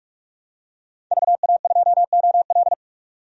Morse code, heard as a keyed tone, about 41 wpm, 700 Hz. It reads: VA2OL
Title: DXCC-100 Call Signs - 5 Characters - 45wpm